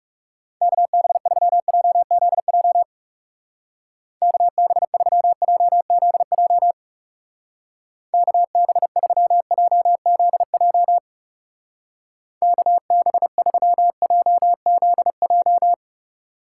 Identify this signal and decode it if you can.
Morse code, heard as a keyed tone, about 37 wpm, 700 Hz. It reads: KB3JZJ KB3JZJ KB3JZJ KB3JZJ
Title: Speed Racing - USA 2x3 Call Signs 30wpm